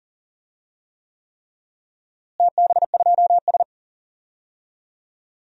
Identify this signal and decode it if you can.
Morse code, heard as a keyed tone, about 40 wpm, 700 Hz. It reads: TB2S
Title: DXCC-100 Call Signs - 4 Characters - Repeated 3x - 40wpm